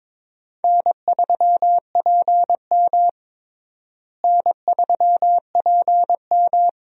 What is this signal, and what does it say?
Morse code, heard as a keyed tone, about 22 wpm, 700 Hz. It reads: N3PM N3PM
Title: USA Call Signs - 4 Characters - Repeated 2x - 22wpm